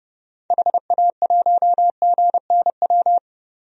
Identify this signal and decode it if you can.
Morse code, heard as a keyed tone, about 32 wpm, 700 Hz. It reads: HA1GNW
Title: DXCC-100 Call Signs - 6 Characters - 30wpm